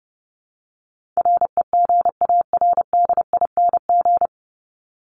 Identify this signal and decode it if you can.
Morse code, heard as a keyed tone, about 30 wpm, 700 Hz. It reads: REGARDING
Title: ING Suffix - Top 100 Words with Base Words 30wpm